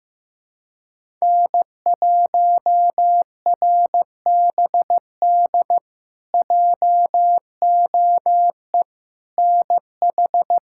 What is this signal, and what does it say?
Morse code, heard as a keyed tone, about 15 wpm, 700 Hz. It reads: N1RBD JOE NH
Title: K1USN SST QSO Practice - 15wpm